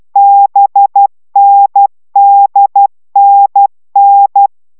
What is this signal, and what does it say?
Morse code, heard as a keyed tone, about 12 wpm, 800 Hz. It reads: BNDNN